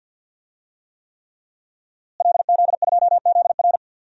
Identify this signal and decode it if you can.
Morse code, heard as a keyed tone, about 47 wpm, 700 Hz. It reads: LZ2ZR
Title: DXCC-100 Call Signs - 5 Characters - Repeated 3x - 50wpm